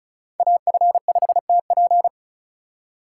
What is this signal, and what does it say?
Morse code, heard as a keyed tone, about 36 wpm, 700 Hz. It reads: AF5TP
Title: USA Call Signs - 5 Characters - 3x Character Spacing - 35wpm